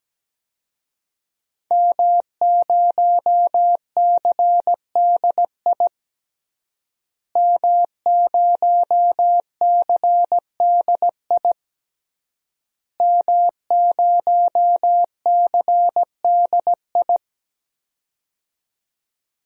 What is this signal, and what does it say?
Morse code, heard as a keyed tone, about 17 wpm, 700 Hz. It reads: M0CDI M0CDI M0CDI
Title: DXCC-100 Call Signs - 5 Characters - Repeated 3x - 17wpm